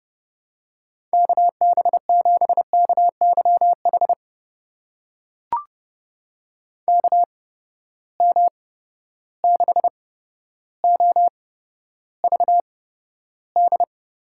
Morse code, heard as a keyed tone, about 32 wpm, spaced as for 13 wpm, 700 Hz. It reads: KB7KYH KM6OVD
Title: USA Call Signs - 6 Characters - 8x Character Spacing - 30wpm